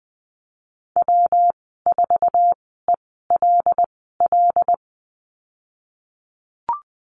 Morse code, heard as a keyed tone, about 20 wpm, spaced as for 15 wpm, 700 Hz. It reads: W4ELL
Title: 167 Lesson 27g - Number 7 - Call Signs 20wpm